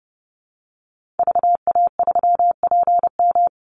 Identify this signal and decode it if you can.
Morse code, heard as a keyed tone, about 30 wpm, 700 Hz. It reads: VA3PM
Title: Call Signs - 30wpm